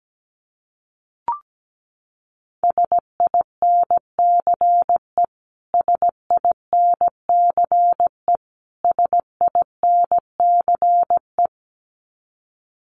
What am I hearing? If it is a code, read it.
Morse code, heard as a keyed tone, about 17 wpm, 700 Hz. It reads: SINCE SINCE SINCE
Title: Top 500 Words Repeated 17wpm